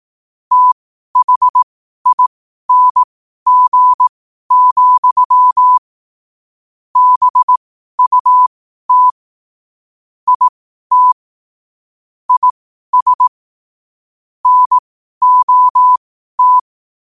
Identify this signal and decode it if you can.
Morse code, heard as a keyed tone, about 18 wpm, spaced as for 12 wpm, 1000 Hz. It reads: THING, BUT IT IS NOT
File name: qotdfiles / 12wpm